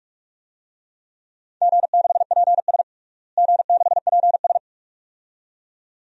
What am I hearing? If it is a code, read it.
Morse code, heard as a keyed tone, about 42 wpm, 700 Hz. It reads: G6PS G6PS
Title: DXCC-100 Call Signs - 4 Characters - Repeated 2x - 45wpm